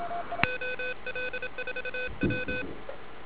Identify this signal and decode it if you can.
Morse code, heard as a keyed tone, about 27 wpm, 3500 Hz, with noise. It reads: OL4W